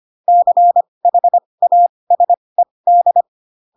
Morse code, heard as a keyed tone, about 25 wpm, spaced as for 20 wpm, 700 Hz. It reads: CHASED
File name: Pinocchio / Pinocchio ch4 15 25 ik1hge